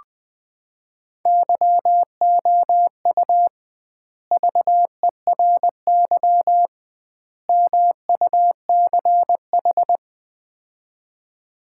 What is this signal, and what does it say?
Morse code, heard as a keyed tone, about 20 wpm, 700 Hz. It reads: YOU VERY MUCH
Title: 2x Word-Spacing - Sets of 3 Words - 20wpm